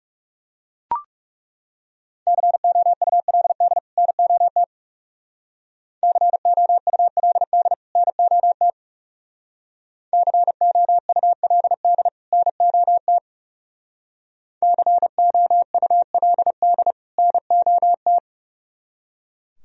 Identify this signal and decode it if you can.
Morse code, heard as a keyed tone, about 35 wpm, 700 Hz. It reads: COULD NOT COULD NOT COULD NOT COULD NOT
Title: Speed-Racing - Sets of 2 Words 30wpm